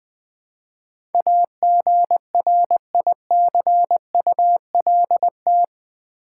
Morse code, heard as a keyed tone, about 20 wpm, 700 Hz. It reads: AGRICULT
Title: Build A Word - 4 Letters At A Time - 12-Letter Words - 20wpm